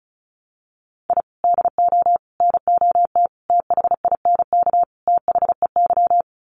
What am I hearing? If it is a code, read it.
Morse code, heard as a keyed tone, about 34 wpm, 700 Hz. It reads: I DO NOT THINK THEY
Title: Sets of 5 Words 35wpm